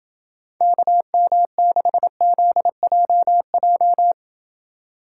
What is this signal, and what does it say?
Morse code, heard as a keyed tone, about 29 wpm, 700 Hz. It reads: KM6ZJJ